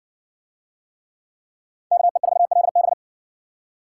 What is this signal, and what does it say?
Morse code, heard as a keyed tone, about 55 wpm, 700 Hz. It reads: KE4LB